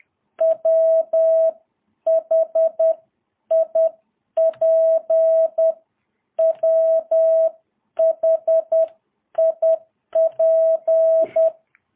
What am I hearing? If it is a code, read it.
Morse code, heard as a keyed tone, about 10 wpm, spaced as for 9 wpm, 650 Hz. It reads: WHIP WHIP
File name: jh8cbh / 2022cw